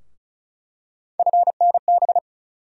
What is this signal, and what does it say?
Morse code, heard as a keyed tone, about 36 wpm, 700 Hz. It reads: FNB
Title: USA Call Sign Postfixes - 3 Letters 35wpm